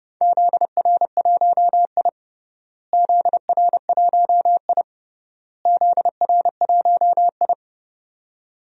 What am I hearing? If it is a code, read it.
Morse code, heard as a keyed tone, about 30 wpm, 700 Hz. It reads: ZR1I ZR1I ZR1I
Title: DXCC-100 Call Signs - 4 Characters - Repeated 3x - 30wpm